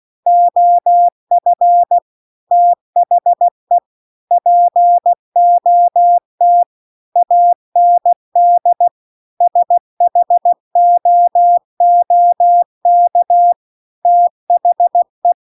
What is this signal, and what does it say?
Morse code, heard as a keyed tone, about 16 wpm, 700 Hz. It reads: OF THE POT AND SHOOK THE